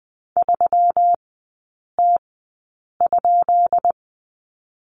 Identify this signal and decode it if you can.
Morse code, heard as a keyed tone, about 20 wpm, spaced as for 8 wpm, 700 Hz. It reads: 3T?
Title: 152 Lesson 25e - Letter p - Three Characters 20wpm